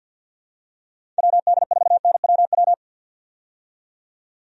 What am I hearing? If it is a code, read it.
Morse code, heard as a keyed tone, about 51 wpm, 700 Hz. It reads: WD4NWW
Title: USA Call Signs - 6 Characters - 50wpm